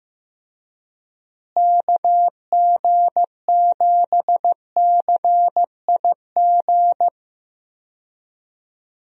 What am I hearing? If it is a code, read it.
Morse code, heard as a keyed tone, about 15 wpm, 700 Hz. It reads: KG7CIG
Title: USA Call Signs - 6 Characters - Repeated 2x - 15wpm